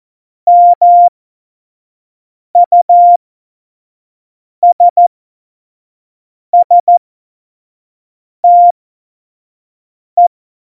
Morse code, heard as a keyed tone, about 14 wpm, spaced as for 5 wpm, 700 Hz. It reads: MUSSTE
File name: Morsekurs / Witze / Witz 22. Mai 2018, 25 F70